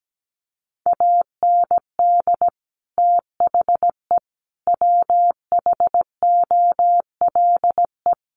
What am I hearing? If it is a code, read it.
Morse code, heard as a keyed tone, about 17 wpm, 700 Hz. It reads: AND THE WHOLE
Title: Sentences from Top 1300 Words 17wpm